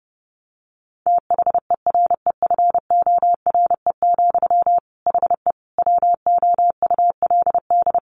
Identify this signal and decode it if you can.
Morse code, heard as a keyed tone, about 30 wpm, 700 Hz. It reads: THEREFORE, HE WOULD
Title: Sentences from Top 800 Words 30wpm